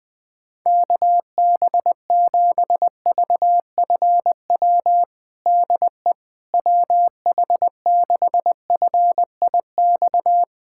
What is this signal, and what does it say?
Morse code, heard as a keyed tone, about 20 wpm, 700 Hz. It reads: KB7VFW DE WH6FIX
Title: USA Callsign DE Callsign - 6 Characters - 20wpm